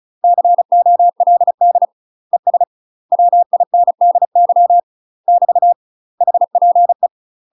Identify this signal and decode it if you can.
Morse code, heard as a keyed tone, about 34 wpm, 700 Hz. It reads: COLD ES WINDY <BT> HPE
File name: UK MASTER QSOs TRAINING/35WPM